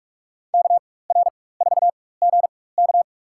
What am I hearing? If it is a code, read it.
Morse code, heard as a keyed tone, about 46 wpm, spaced as for 21 wpm, 700 Hz. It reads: KR4GK
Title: USA Call Signs - 5 Characters - 4x Character Spacing - 45wpm